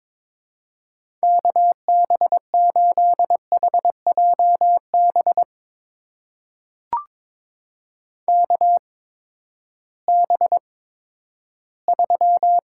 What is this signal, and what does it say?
Morse code, heard as a keyed tone, about 22 wpm, spaced as for 13 wpm, 700 Hz. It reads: KB8HJB KB3
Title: USA Call Signs - 6 Characters - 8x Character Spacing - 22wpm